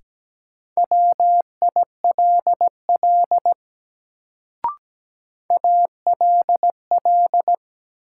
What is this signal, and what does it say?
Morse code, heard as a keyed tone, about 17 wpm, 700 Hz. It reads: WILL ALL
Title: Sets of 2 Words - New Word Drill - 17wpm